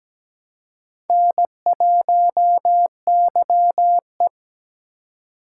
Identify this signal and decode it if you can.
Morse code, heard as a keyed tone, about 17 wpm, 700 Hz. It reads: N1YE
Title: K1USN SST QSO Practice - 17wpm